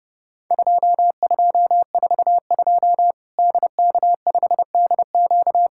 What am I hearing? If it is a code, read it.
Morse code, heard as a keyed tone, about 30 wpm, 700 Hz. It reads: 2242 DK5DQ
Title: Worked All Europe DX Contest - QTC Exchange - 30wpm